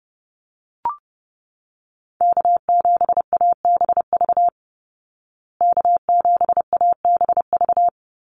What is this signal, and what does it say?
Morse code, heard as a keyed tone, about 30 wpm, 700 Hz. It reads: K7ABV K7ABV
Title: Call Signs Repeated 3x - 30wpm